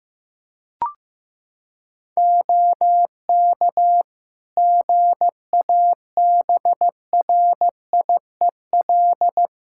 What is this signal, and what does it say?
Morse code, heard as a keyed tone, about 15 wpm, 700 Hz. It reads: OK GABRIEL
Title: Practice QSO - Exchange 4 - Ending - 15wpm